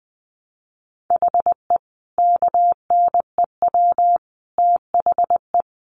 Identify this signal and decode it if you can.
Morse code, heard as a keyed tone, about 20 wpm, 700 Hz. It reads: HE KNEW THE
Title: Sentences from Top 1300 Words 20wpm